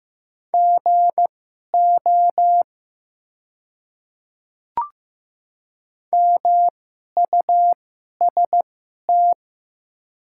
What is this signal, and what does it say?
Morse code, heard as a keyed tone, about 15 wpm, spaced as for 11 wpm, 700 Hz. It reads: GO MUST
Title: 2x Character Spacing - Top 300 Words - 15wpm